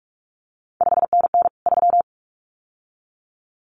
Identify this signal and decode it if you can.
Morse code, heard as a keyed tone, about 46 wpm, 700 Hz. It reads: 5NN 3
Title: CQ WW DX Contest Exchange - 45wpm